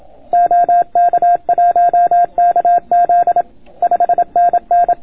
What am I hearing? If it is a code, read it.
Morse code, heard as a keyed tone, about 26 wpm, 690 Hz. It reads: OK1KZ 5NN